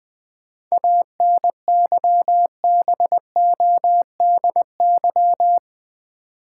Morse code, heard as a keyed tone, about 20 wpm, 700 Hz. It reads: ANYBODY
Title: Sets of 5 Words - New Word Drill - 20wpm